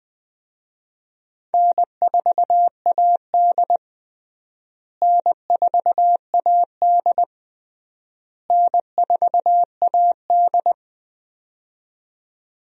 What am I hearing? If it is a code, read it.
Morse code, heard as a keyed tone, about 20 wpm, 700 Hz. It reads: N4AD N4AD N4AD
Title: USA Call Signs - 4 Characters - Repeated 3x - 20wpm